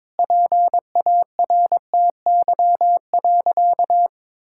Morse code, heard as a keyed tone, about 22 wpm, 700 Hz. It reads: PARTY.
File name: practice / 22wpm / Sentences with 2-Word Phrases - 22wpm